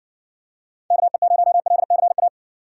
Morse code, heard as a keyed tone, about 60 wpm, 700 Hz. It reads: KE0LCA